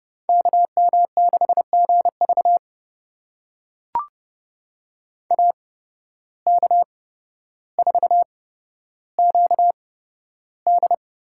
Morse code, heard as a keyed tone, about 30 wpm, spaced as for 13 wpm, 700 Hz. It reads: KM6GV AK4QD
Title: USA Call Signs - 5 Characters - 8x Character Spacing - 30wpm